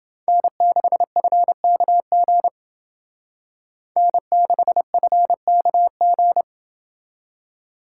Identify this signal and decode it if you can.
Morse code, heard as a keyed tone, about 28 wpm, 700 Hz. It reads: N6FKG N6FKG